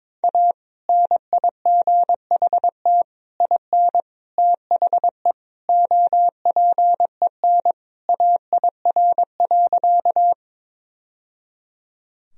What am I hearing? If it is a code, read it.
Morse code, heard as a keyed tone, about 21 wpm, 700 Hz. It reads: A NIGHT IN THE OPEN AIR.
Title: Sentences from Top 1000 Words - Review Part 1 - 22wpm